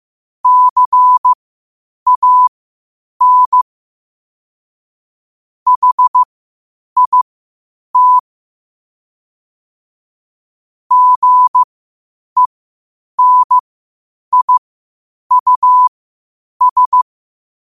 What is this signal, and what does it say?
Morse code, heard as a keyed tone, about 15 wpm, spaced as for 8 wpm, 1000 Hz. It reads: CAN HIT GENIUS
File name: qotdfiles / 8wpm